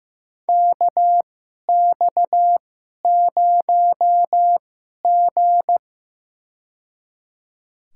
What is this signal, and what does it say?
Morse code, heard as a keyed tone, about 15 wpm, spaced as for 11 wpm, 700 Hz. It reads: KX0G
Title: USA Call Signs - 2x Character Spacing - 15wpm